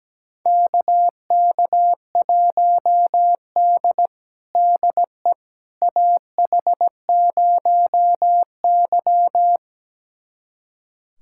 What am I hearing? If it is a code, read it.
Morse code, heard as a keyed tone, about 17 wpm, 700 Hz. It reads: KK1D DE AH0Y
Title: USA Callsign DE Callsign - 4 Characters - 17wpm